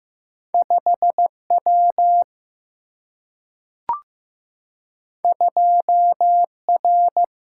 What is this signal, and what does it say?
Morse code, heard as a keyed tone, about 15 wpm, 700 Hz. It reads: 5W 2R
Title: Single-Letter Single-Number 15wpm